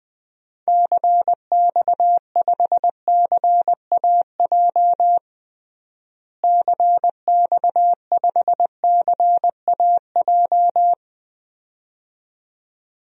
Morse code, heard as a keyed tone, about 20 wpm, 700 Hz. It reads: CX5CAJ CX5CAJ
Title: DXCC-100 Call Signs - 6 Characters - Repeated 2x - 20wpm